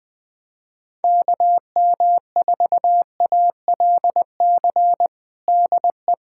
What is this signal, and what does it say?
Morse code, heard as a keyed tone, about 20 wpm, 700 Hz. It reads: KM4ALC DE
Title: USA Callsign DE Callsign 20wpm